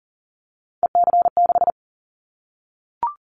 Morse code, heard as a keyed tone, about 42 wpm, 700 Hz. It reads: EC6
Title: Callsign Prefixes for Top 50 DXCC - 40wpm